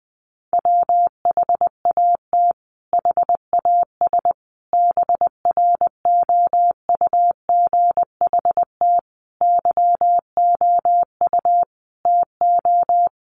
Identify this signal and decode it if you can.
Morse code, heard as a keyed tone, about 20 wpm, 700 Hz. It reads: WHAT HAS BROUGHT YOU TO
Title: Sentences from Top 1500 Words 20wpm